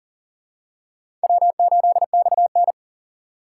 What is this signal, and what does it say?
Morse code, heard as a keyed tone, about 40 wpm, 700 Hz. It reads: W8XN
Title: USA Call Signs - 4 Characters - 3x Character Spacing - 40wpm